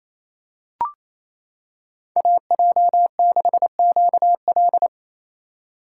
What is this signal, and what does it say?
Morse code, heard as a keyed tone, about 27 wpm, 700 Hz. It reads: AJ6QL
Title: K1USN SST QSO Practice - 28wpm